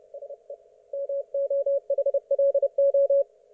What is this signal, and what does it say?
Morse code, heard as a keyed tone, about 31 wpm, 560 Hz. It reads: SE MOHLO
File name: Archiv / CUC179 / ok1fhi cul179